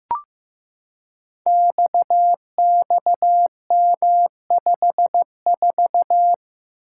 Morse code, heard as a keyed tone, about 15 wpm, 700 Hz. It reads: XXM54
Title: Code Groups - 5 Letter:Numbers - 15wpm